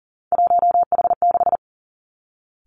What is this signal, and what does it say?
Morse code, heard as a keyed tone, about 38 wpm, 690 Hz. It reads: 1H6
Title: Three Challenging Letter-Number Combinations 40wpm